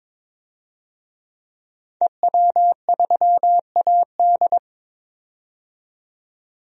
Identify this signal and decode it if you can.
Morse code, heard as a keyed tone, about 22 wpm, 700 Hz. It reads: EW3AD
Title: DXCC-100 Call Signs - 5 Characters - 22wpm